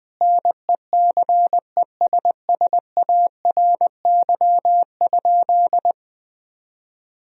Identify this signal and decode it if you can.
Morse code, heard as a keyed tone, about 20 wpm, 700 Hz. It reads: NECESSARY?
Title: Sentences from Top 1000 Words - Review Part 3 - 20wpm